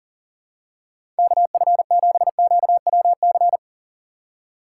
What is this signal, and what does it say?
Morse code, heard as a keyed tone, about 40 wpm, 700 Hz. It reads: KF7QWC
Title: USA Call Signs - 6x Character Spacing - 40wpm